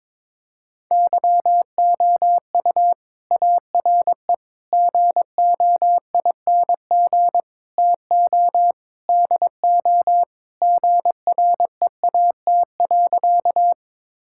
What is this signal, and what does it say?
Morse code, heard as a keyed tone, about 22 wpm, 700 Hz. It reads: YOU ARE GOING TO DO GREAT.
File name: practice / 22wpm / Sentences with 4-Word Phrases - 22wpm